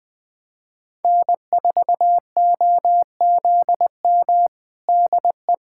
Morse code, heard as a keyed tone, about 20 wpm, 700 Hz. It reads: N4OZM DE
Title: USA Callsign DE Callsign - 5 Characters - 20wpm